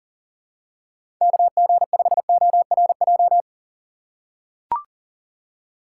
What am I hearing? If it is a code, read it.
Morse code, heard as a keyed tone, about 40 wpm, 700 Hz. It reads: KG5ORJ
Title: USA Call Signs - 6 Characters - 3x Character Spacing - 40wpm